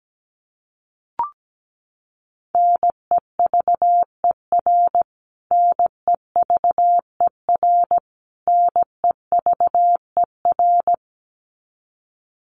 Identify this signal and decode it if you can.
Morse code, heard as a keyed tone, about 17 wpm, 700 Hz. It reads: NEVER NEVER NEVER
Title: Top 500 Words Repeated 17wpm